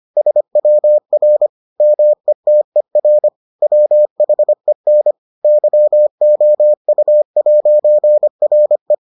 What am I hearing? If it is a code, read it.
Morse code, heard as a keyed tone, about 25 wpm, 600 Hz. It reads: SWR METER WHEN YOU'RE